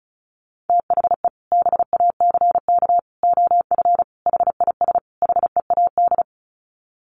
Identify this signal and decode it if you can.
Morse code, heard as a keyed tone, about 34 wpm, 700 Hz. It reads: THE BACK OF HIS HEAD
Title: Sets of 5 Words Encore 35wpm